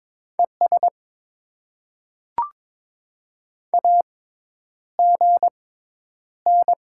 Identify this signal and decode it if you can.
Morse code, heard as a keyed tone, about 22 wpm, spaced as for 10 wpm, 700 Hz. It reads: ES AGN
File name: Top 20 QSO Elements - 6x Character Spacing - 22wpm